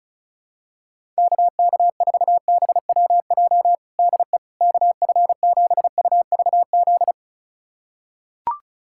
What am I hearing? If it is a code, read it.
Morse code, heard as a keyed tone, about 36 wpm, 700 Hz. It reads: KK4BWJ DE KF7UVZ
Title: USA Callsign DE Callsign - 6 Characters - 35wpm